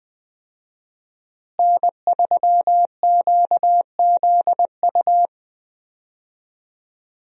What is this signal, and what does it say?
Morse code, heard as a keyed tone, about 20 wpm, 700 Hz. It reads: N3QZU
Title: USA Call Signs - 5 Characters - Repeated 3x - 20wpm